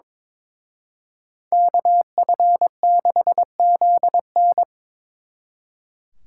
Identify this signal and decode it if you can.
Morse code, heard as a keyed tone, about 26 wpm, 700 Hz. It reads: KF6ZN
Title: Speed Racing - USA Call Signs 22wpm